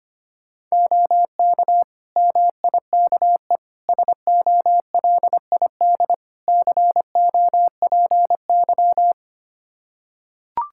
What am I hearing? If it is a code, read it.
Morse code, heard as a keyed tone, about 25 wpm, 700 Hz. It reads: OK MIKE SOLID COPY
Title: Practice QSO - Exchange 3 - More Info - 25wpm